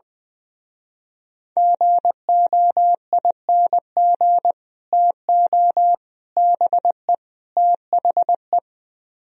Morse code, heard as a keyed tone, about 23 wpm, 700 Hz. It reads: GOING TO BE THE
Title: Speed-Racing - Sets of 4 Words 20wpm